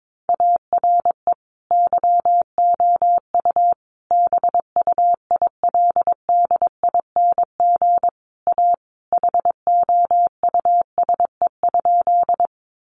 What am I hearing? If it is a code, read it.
Morse code, heard as a keyed tone, about 22 wpm, 700 Hz. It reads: ARE YOU BUILDING A HOUSE?
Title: Sentences from Top 700 Words 22wpm